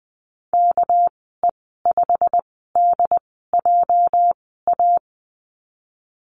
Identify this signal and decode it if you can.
Morse code, heard as a keyed tone, about 20 wpm, spaced as for 14 wpm, 700 Hz. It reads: KE5DJA
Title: 213 Lesson 34g - Number 8 - Call Signs 20wpm